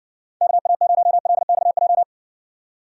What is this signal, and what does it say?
Morse code, heard as a keyed tone, about 60 wpm, 700 Hz. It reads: KA0LXJ